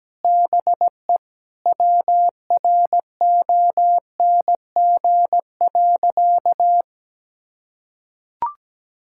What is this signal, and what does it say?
Morse code, heard as a keyed tone, about 17 wpm, 700 Hz. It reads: BE WRONG.
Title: Sentences from Top 1600 Words 17wpm